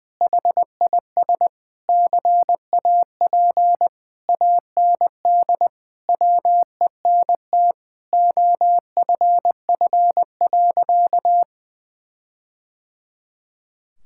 Morse code, heard as a keyed tone, about 21 wpm, 700 Hz. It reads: HIS CAP AND WENT OFF.
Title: Sentences from Top 1600 Words 20wpm